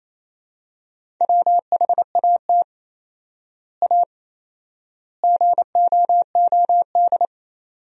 Morse code, heard as a keyed tone, about 28 wpm, 700 Hz. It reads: WHAT A GOOD
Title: 4x Word-Spacing - Sentences from Top 200 Words - 28wpm